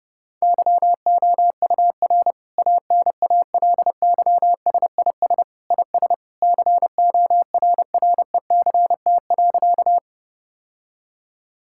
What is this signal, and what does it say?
Morse code, heard as a keyed tone, about 32 wpm, 700 Hz. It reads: YOUR ANALYSIS IS CORRECT.
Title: Sentences from Top 1000 Words - Review Part 5 - 30wpm